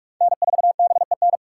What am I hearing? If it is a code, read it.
Morse code, heard as a keyed tone, about 46 wpm, 700 Hz. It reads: N4BEN
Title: USA Call Signs - 6x Character Spacing - 45wpm